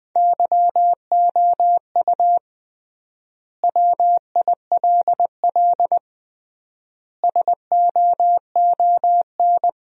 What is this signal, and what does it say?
Morse code, heard as a keyed tone, about 20 wpm, 700 Hz. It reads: YOU WILL SOON
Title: 3x Word-Spacing - Sentences from Top 400 Words - 20wpm